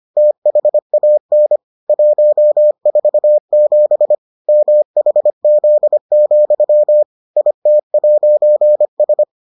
Morse code, heard as a keyed tone, about 25 wpm, 600 Hz. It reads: THAN 147 MHZ, IT'S